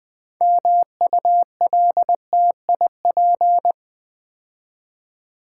Morse code, heard as a keyed tone, about 20 wpm, 700 Hz. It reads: MULTIP
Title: Build A Word - 2 Letters At A Time - 8-Letter Words - 20wpm